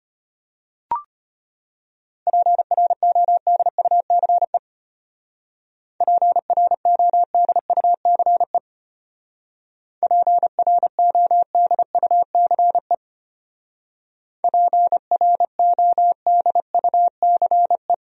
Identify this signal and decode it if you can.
Morse code, heard as a keyed tone, about 31 wpm, 700 Hz. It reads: PRODUCE PRODUCE PRODUCE PRODUCE
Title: Speed-Racing Top 500 Words 25wpm